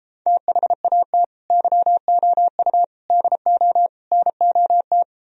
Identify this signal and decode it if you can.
Morse code, heard as a keyed tone, about 33 wpm, 700 Hz. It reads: THAT YOU DO NOT